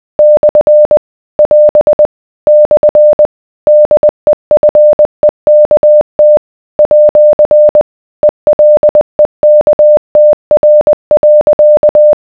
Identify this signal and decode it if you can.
Morse code, heard as a keyed tone, about 20 wpm, 600 Hz. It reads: / <AS> / DEFEKT @ ELEKTR.